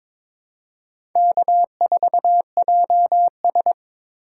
Morse code, heard as a keyed tone, about 21 wpm, 700 Hz. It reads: K4JS
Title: USA Call Signs - 4 Characters - 4x Character Spacing - 22wpm